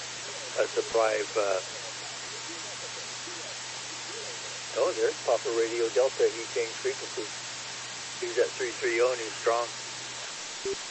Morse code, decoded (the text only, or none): none